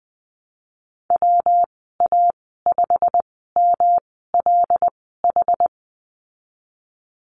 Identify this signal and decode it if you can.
Morse code, heard as a keyed tone, about 20 wpm, spaced as for 15 wpm, 700 Hz. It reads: WA5MLH
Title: 128 Lesson 21g - Number 6 - Call Signs 20wpm